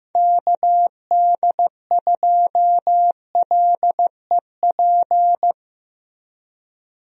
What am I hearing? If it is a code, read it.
Morse code, heard as a keyed tone, about 15 wpm, 700 Hz. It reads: KD2LEP
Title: USA Call Signs - 6 Characters - Repeated 3x - 15wpm